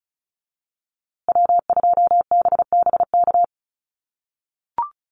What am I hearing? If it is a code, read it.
Morse code, heard as a keyed tone, about 34 wpm, 700 Hz. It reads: W2BBK
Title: USA Call Signs Repeated 2x 35wpm